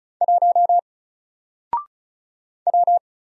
Morse code, heard as a keyed tone, about 36 wpm, 700 Hz. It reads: 1 W
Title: Single Letter-Number Alt 35wpm